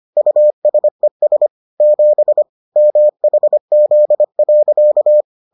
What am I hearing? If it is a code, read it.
Morse code, heard as a keyed tone, about 25 wpm, 600 Hz. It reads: USES 7 MHZ.